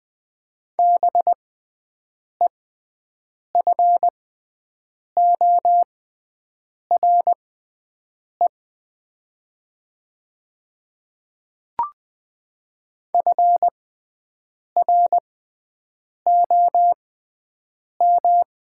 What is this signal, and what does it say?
Morse code, heard as a keyed tone, about 20 wpm, spaced as for 7 wpm, 700 Hz. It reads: BEFORE FROM
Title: 6x Character Spacing - Top 300 Words - 20wpm